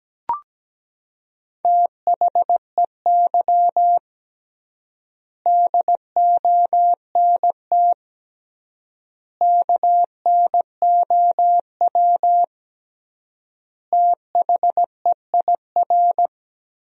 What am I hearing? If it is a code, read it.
Morse code, heard as a keyed tone, about 17 wpm, 700 Hz. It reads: THEY DONT KNOW THEIR
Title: 3x Word-Spacing - Sentences from Top 300 Words - 17wpm